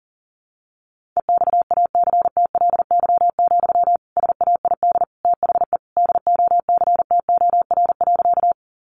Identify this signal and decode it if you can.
Morse code, heard as a keyed tone, about 40 wpm, 700 Hz. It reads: EXACTLY, SAID THE DOCTOR.
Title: Sentences from Top 1500 Words 40wpm